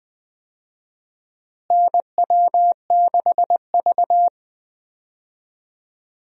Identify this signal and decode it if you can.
Morse code, heard as a keyed tone, about 20 wpm, 700 Hz. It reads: NW6V
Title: USA Call Signs - 4 Characters - Repeated 3x - 20wpm